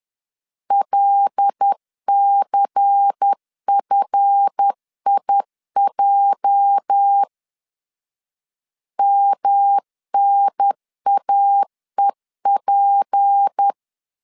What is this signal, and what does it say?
Morse code, heard as a keyed tone, about 10 wpm, 800 Hz. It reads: LCFIJ MNAEP
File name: CW / DET / 1 9 1500C S B 10wpm 4439